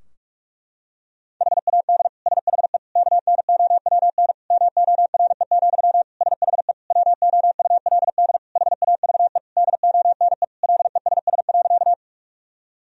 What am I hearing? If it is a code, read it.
Morse code, heard as a keyed tone, about 46 wpm, 700 Hz. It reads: HAD SHE KNOWN MORE, SHE WOULD HAVE DONE LESS.
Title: Sentences from Top 1000 Words - Review Part 3 - 45wpm